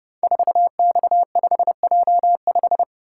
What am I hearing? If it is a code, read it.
Morse code, heard as a keyed tone, about 28 wpm, 700 Hz. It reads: 4X5J5
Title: Code Groups - 5 Letter:Numbers - 30wpm